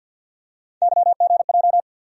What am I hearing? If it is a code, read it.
Morse code, heard as a keyed tone, about 53 wpm, 700 Hz. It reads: YGJ
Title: USA Call Sign Postfixes - 3 Letters 50wpm